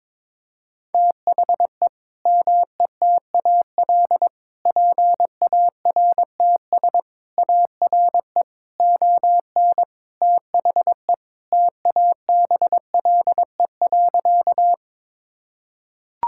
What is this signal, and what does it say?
Morse code, heard as a keyed tone, about 21 wpm, 700 Hz. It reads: THE METAL PARTS ARE ON THE TABLE.
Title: Sentences from Top 1000 Words - Review Part 6 - 22wpm